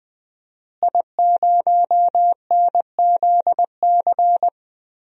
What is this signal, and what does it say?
Morse code, heard as a keyed tone, about 21 wpm, 700 Hz. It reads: I0NZC
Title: DXCC-100 Call Signs - 5 Characters - Repeated 3x - 20wpm